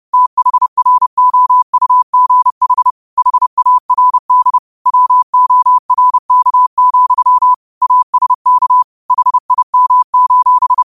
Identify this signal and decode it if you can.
Morse code, heard as a keyed tone, about 30 wpm, 1000 Hz. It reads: THROUGH HARD WORK, ASK HIM: